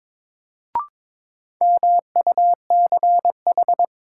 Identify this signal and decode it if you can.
Morse code, heard as a keyed tone, about 22 wpm, 700 Hz. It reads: MUCH
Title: Sentences from Top 200 Words - New Word Drill - 22wpm